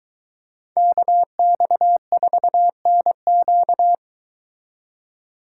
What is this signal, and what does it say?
Morse code, heard as a keyed tone, about 23 wpm, 700 Hz. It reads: KX4NQ